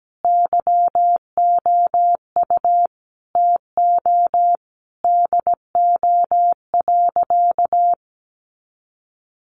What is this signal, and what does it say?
Morse code, heard as a keyed tone, about 17 wpm, 700 Hz. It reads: YOU TO DO.
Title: Sentences from Top 1300 Words 17wpm